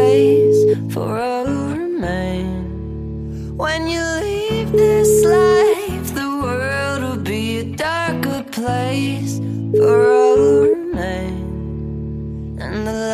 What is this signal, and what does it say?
Ringback tone (Europe, 425 Hz), with noise.